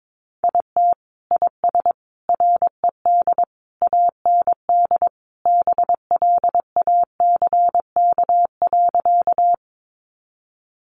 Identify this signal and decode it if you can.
Morse code, heard as a keyed tone, about 22 wpm, 700 Hz. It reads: IT IS RED AND BLACK.
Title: Easy Phrases 22wpm